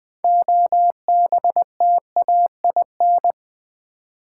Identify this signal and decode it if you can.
Morse code, heard as a keyed tone, about 21 wpm, 700 Hz. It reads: OBTAIN
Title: 6 Letter Words - First 3 Letters Given - 20wpm